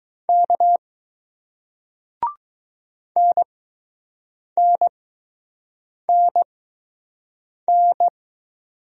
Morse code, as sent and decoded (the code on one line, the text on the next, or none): -.- / -. -. -. -.
K NNNN